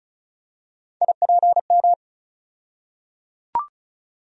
Morse code, heard as a keyed tone, about 34 wpm, 700 Hz. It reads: IPM
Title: USA Call Sign Postfixes - 3 Letters 35wpm